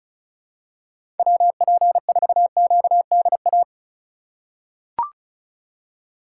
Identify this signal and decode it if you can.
Morse code, heard as a keyed tone, about 36 wpm, 700 Hz. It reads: WP4QDA
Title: USA Call Signs - 3x Character Spacing - 35wpm